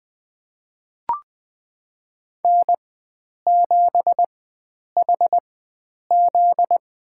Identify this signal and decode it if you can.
Morse code, heard as a keyed tone, about 20 wpm, spaced as for 9 wpm, 700 Hz. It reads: N7HZ
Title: USA Call Signs - 4x Character Spacing - 20wpm